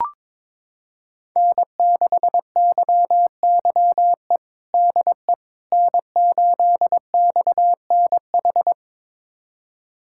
Morse code, heard as a keyed tone, about 22 wpm, 700 Hz. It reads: N6YYE DE N8XNH
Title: USA Callsign DE Callsign - 5 Characters - 22wpm